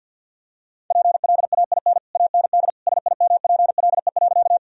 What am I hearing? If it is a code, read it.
Morse code, heard as a keyed tone, about 52 wpm, 700 Hz. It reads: PLAIN AND SIMPLE.
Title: Sentences from Top 1000 Words - Review Part 2 - 50wpm